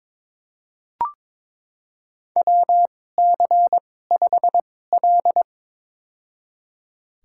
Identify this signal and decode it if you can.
Morse code, heard as a keyed tone, about 22 wpm, spaced as for 16 wpm, 700 Hz. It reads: WC5L
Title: USA Call Signs - 4 Characters - 2x Character Spacing - 22wpm